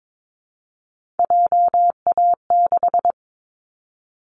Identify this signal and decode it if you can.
Morse code, heard as a keyed tone, about 22 wpm, 700 Hz. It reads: JA6
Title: Callsign Prefixes for Top 25 DXCC - 22wpm